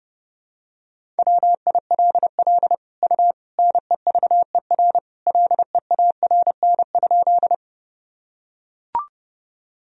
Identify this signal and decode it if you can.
Morse code, heard as a keyed tone, about 32 wpm, 700 Hz. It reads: WILL U NEVER LEARN?
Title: Sentences from Top 300 Words and Abbreviations 30wpm